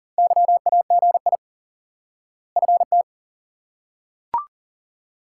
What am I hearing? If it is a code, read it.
Morse code, heard as a keyed tone, about 40 wpm, 700 Hz. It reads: YAGI FT
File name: IQE Course 3 - Lesson 89 - YAGI - Two Elements - 40wpm